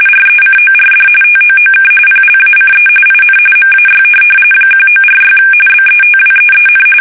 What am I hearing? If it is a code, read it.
RTTY, about 45.45 baud. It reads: RECORD TOOK PLACE IN MAY 1946 BETWEEN
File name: fsk850 / teletype-RTTY